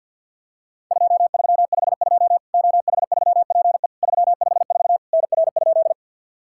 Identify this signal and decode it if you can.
Morse code, heard as a keyed tone, about 50 wpm, 690 Hz. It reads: 2352 OH3PE 354 NR?
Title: Worked All Europe DX Contest - QTC Exchange with Repeats - 50wpm